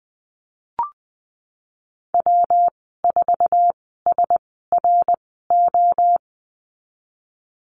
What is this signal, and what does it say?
Morse code, heard as a keyed tone, about 20 wpm, spaced as for 15 wpm, 700 Hz. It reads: W4SRO
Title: 173 Lesson 28g - Number 9 - Call Signs 20wpm